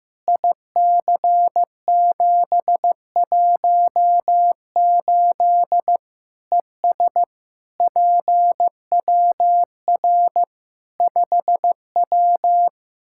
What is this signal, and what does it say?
Morse code, heard as a keyed tone, about 15 wpm, 700 Hz. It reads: IC718 ES PWR 5W
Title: Practice QSO - Exchange 2 - Info - 15wpm